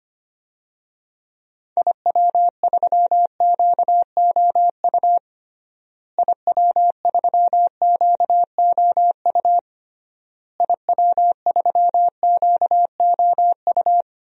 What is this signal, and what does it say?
Morse code, heard as a keyed tone, about 25 wpm, 700 Hz. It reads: IW3QOU IW3QOU IW3QOU
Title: DXCC-100 Call Signs - 6 Characters - Repeated 3x - 25wpm